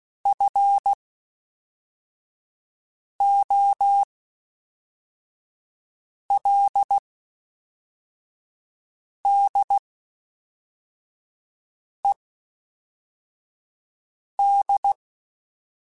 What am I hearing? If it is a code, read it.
Morse code, heard as a keyed tone, about 16 wpm, spaced as for 4 wpm, 800 Hz. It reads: FOLDED
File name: (spaced) / FromSMs spaced mp3s / Antennas sp@5